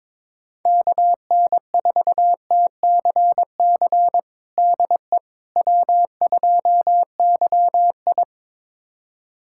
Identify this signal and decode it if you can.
Morse code, heard as a keyed tone, about 21 wpm, 700 Hz. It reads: KN4TCC DE W2YI
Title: USA Callsign DE Callsign 22wpm